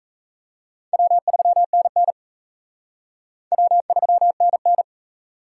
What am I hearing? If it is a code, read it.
Morse code, heard as a keyed tone, about 34 wpm, 700 Hz. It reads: W3NN W3NN
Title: Speed Racing - USA 1x2 Call Signs 28wpm